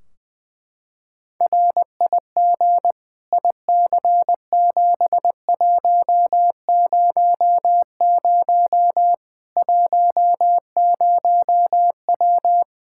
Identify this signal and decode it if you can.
Morse code, heard as a keyed tone, about 20 wpm, 700 Hz. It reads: RIG IC7100 10W
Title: Practice QSO - Exchange 2 - Info - 20wpm